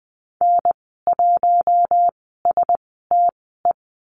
Morse code, heard as a keyed tone, about 20 wpm, spaced as for 15 wpm, 700 Hz. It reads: N1STE
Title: 055 Lesson 10g - Letter r - Call Signs 20wpm